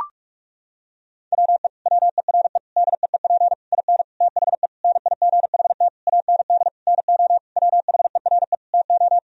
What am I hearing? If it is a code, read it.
Morse code, heard as a keyed tone, about 41 wpm, 700 Hz. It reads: WE WERE DEEP IN THE NIGHT AND NO WHERE TO
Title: Sentences from Top 1000 Words - Review Part 3 - 45wpm